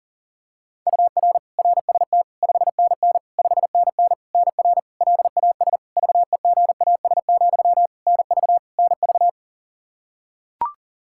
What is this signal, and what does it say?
Morse code, heard as a keyed tone, about 40 wpm, 700 Hz. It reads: UR RST 5NN 5NN NR LAS VEGAS, NV NV
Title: K1USN SST QSO Practice 40wpm